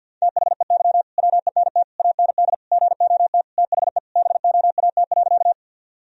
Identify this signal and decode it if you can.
Morse code, heard as a keyed tone, about 51 wpm, 700 Hz. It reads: THEY WENT AND GOT THE BOAT.
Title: Sentences from Top 1000 Words - Review Part 3 - 50wpm